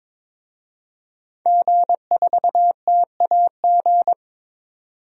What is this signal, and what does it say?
Morse code, heard as a keyed tone, about 22 wpm, 700 Hz. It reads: G4TAG
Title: DXCC-100 Call Signs - 5 Characters - 22wpm